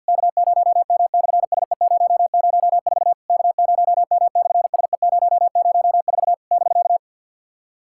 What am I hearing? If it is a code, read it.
Morse code, heard as a keyed tone, about 47 wpm, 690 Hz. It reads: K0M/SE004 K0M/SE004 <BK>
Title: SOTA QSO Practice 50wpm